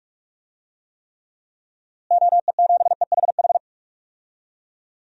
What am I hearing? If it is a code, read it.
Morse code, heard as a keyed tone, about 41 wpm, 700 Hz. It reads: OE7EHH
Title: DXCC-100 Call Signs - 6 Characters - Repeated 3x - 45wpm